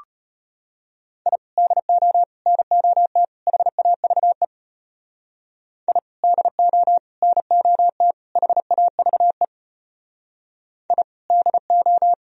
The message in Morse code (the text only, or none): I DO NOT HAVE I DO NOT HAVE I DO